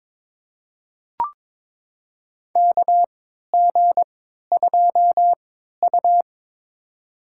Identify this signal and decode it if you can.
Morse code, heard as a keyed tone, about 21 wpm, spaced as for 12 wpm, 700 Hz. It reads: KG2U
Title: USA Call Signs - 4 Characters - 3x Character Spacing - 22wpm